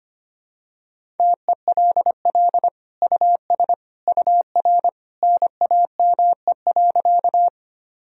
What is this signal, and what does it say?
Morse code, heard as a keyed tone, about 25 wpm, 700 Hz. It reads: TELL US UR NAME.
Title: Sentences from Top 200 Words and Abbreviations 25wpm